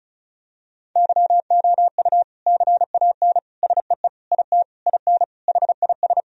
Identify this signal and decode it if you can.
Morse code, heard as a keyed tone, about 34 wpm, 700 Hz. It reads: YOU CAN SEE IT IN HIS
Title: Sentences from Top 1000 Words - Review Part 1 - 35wpm